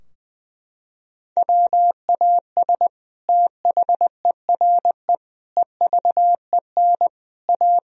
Morse code, heard as a keyed tone, about 21 wpm, 700 Hz. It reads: WAS THERE EVEN A
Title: Sentences from Top 1000 Words - Review Part 4 - 20wpm